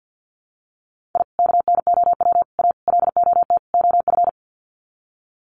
Morse code, heard as a keyed tone, about 50 wpm, 700 Hz. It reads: I KNOW A LOT OF
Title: Sets of 5 Words Encore 50wpm